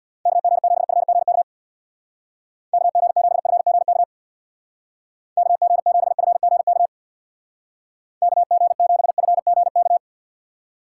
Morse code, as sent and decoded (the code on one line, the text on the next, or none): -.- --. --... ..-. --. -.- / -.- --. --... ..-. --. -.- / -.- --. --... ..-. --. -.- / -.- --. --... ..-. --. -.-
KG7FGK KG7FGK KG7FGK KG7FGK